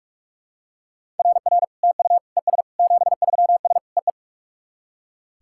Morse code, heard as a keyed tone, about 44 wpm, 700 Hz. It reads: RR TU ES 73S EE
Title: SOTA QSO Practice 45wpm